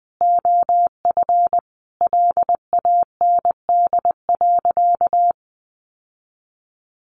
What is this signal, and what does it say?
Morse code, heard as a keyed tone, about 20 wpm, 700 Hz. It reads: OF LAND.
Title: Sentences from Top 1300 Words 20wpm